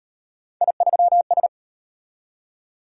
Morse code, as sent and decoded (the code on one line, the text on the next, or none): .. ...-- ...
I3S